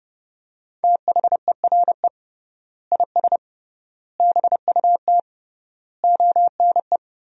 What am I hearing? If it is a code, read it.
Morse code, heard as a keyed tone, about 30 wpm, 700 Hz. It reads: THERE IS BUT ONE
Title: 3x Word-Spacing - Sentences from Top 100 Words - 30wpm